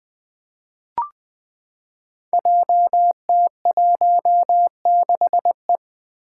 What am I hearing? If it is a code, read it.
Morse code, heard as a keyed tone, about 20 wpm, 700 Hz. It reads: JT16E
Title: Five Challenging Letter-Number Combinations 20wpm